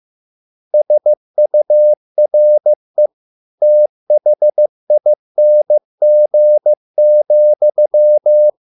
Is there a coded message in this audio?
Yes: Morse code from 0.7 to 8.5 s, heard as a keyed tone, about 15 wpm, 600 Hz. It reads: SURE THING,